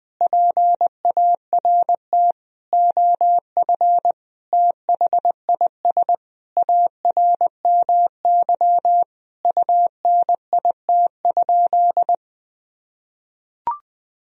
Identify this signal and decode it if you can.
Morse code, heard as a keyed tone, about 20 wpm, 700 Hz. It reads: PART OF THIS ARMY UNIT?
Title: Sentences from Top 1000 Words - Review Part 2 - 20wpm